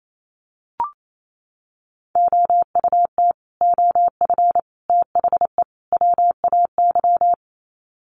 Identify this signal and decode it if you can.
Morse code, heard as a keyed tone, about 28 wpm, 700 Hz. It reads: OUT OF THE WAY
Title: Sets of 4 Words 28wpm